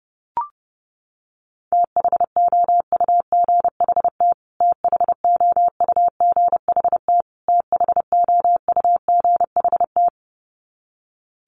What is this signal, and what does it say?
Morse code, heard as a keyed tone, about 28 wpm, 700 Hz. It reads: THOUGHT THOUGHT THOUGHT
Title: Top 500 Words Repeated 30wpm